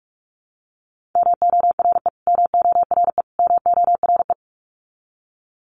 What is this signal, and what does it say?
Morse code, heard as a keyed tone, about 44 wpm, 700 Hz. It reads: MORE MORE MORE
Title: Top 300 Words Repeated 45wpm